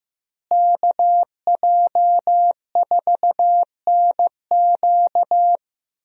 Morse code, heard as a keyed tone, about 15 wpm, 700 Hz. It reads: KJ4NQ
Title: USA Call Signs - 5 Characters - Repeated 3x - 15wpm